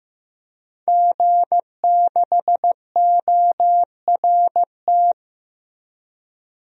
Morse code, heard as a keyed tone, about 15 wpm, 700 Hz. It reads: G6ORT
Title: DXCC-100 Call Signs - 5 Characters - 15wpm